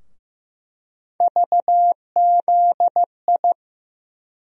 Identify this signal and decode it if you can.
Morse code, heard as a keyed tone, about 15 wpm, 700 Hz. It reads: VZI
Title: USA Call Sign Postfixes - 3 Letters 15wpm